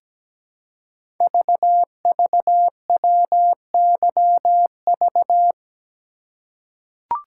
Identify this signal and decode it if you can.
Morse code, heard as a keyed tone, about 17 wpm, 700 Hz. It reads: VVWYV
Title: Code Groups - 5 Letters - 17wpm